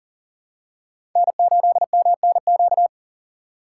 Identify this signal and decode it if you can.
Morse code, heard as a keyed tone, about 40 wpm, 700 Hz. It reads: N8MNQ
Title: USA Call Signs - 5 Characters - 3x Character Spacing - 40wpm